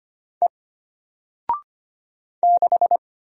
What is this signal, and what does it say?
Morse code, heard as a keyed tone, about 25 wpm, 700 Hz. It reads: E 6
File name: Single-Letter-Number-25wpm